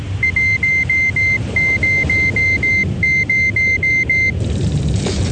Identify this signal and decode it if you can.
There is no code.